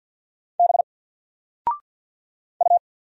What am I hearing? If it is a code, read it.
Morse code, heard as a keyed tone, about 50 wpm, 700 Hz. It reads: B U
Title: Single Letter-Number 50wpm